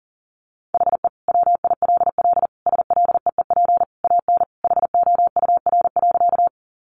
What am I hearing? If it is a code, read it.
Morse code, heard as a keyed tone, about 40 wpm, 700 Hz. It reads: HE WILL SLEEP AN HOUR.
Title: Sentences from Top 500 Words 40wpm